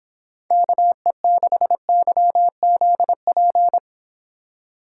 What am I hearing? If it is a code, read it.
Morse code, heard as a keyed tone, about 26 wpm, 700 Hz. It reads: KE6YZP